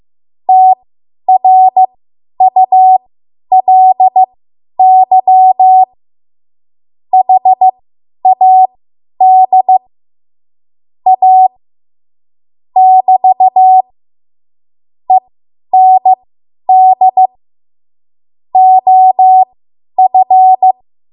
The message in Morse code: TRULY HAD A <BT> END OF

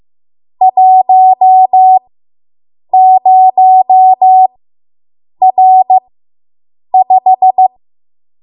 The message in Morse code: 10R5

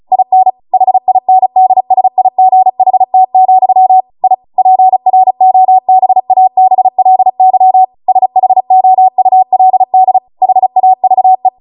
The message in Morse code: IN HINDSIGHT, I PROBABLY SHOULD HAVE

